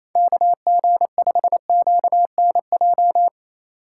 Morse code, heard as a keyed tone, about 28 wpm, 700 Hz. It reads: KG5QNJ
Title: USA Call Signs - 3x Character Spacing - 28wpm